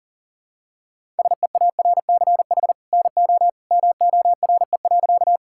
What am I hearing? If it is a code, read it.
Morse code, heard as a keyed tone, about 40 wpm, 700 Hz. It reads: SEARCH NO MORE.
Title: Sentences from Top 1000 Words - Review Part 5 - 40wpm